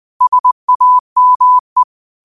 Morse code, heard as a keyed tone, about 20 wpm, 1000 Hz. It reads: SAME